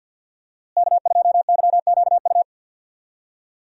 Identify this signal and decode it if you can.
Morse code, heard as a keyed tone, about 46 wpm, 690 Hz. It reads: K2YQU
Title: USA Call Signs - 5 Characters - Repeated 3x - 50wpm